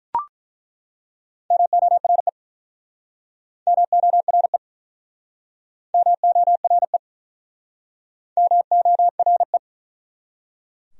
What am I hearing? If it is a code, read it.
Morse code, heard as a keyed tone, about 47 wpm, 700 Hz. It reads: MORE MORE MORE MORE